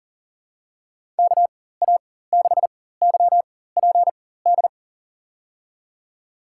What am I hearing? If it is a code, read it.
Morse code, heard as a keyed tone, about 40 wpm, spaced as for 19 wpm, 700 Hz. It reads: KA6YPD
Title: USA Call Signs - 6 Characters - 4x Character Spacing - 40wpm